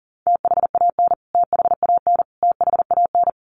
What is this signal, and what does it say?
Morse code, heard as a keyed tone, about 40 wpm, 700 Hz. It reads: THAN THAN THAN
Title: Top 500 Words Repeated 40wpm